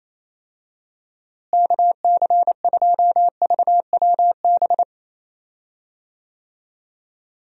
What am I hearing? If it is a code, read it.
Morse code, heard as a keyed tone, about 28 wpm, 700 Hz. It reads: KC2VWB
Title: USA Call Signs - 6 Characters - Repeated 3x - 28wpm